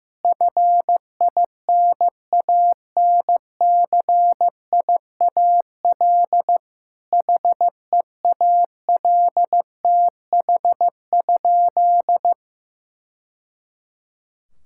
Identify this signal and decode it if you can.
Morse code, heard as a keyed tone, about 15 wpm, 700 Hz. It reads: FINANCIAL HEALTH?
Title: Sentences from Top 1000 Words - Review Part 5 - 15wpm